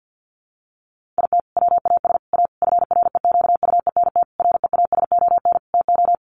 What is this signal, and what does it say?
Morse code, heard as a keyed tone, about 48 wpm, 700 Hz. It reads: IT WAS A FREQUENT REASON TO
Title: Sentences from Top 1400 Words 50wpm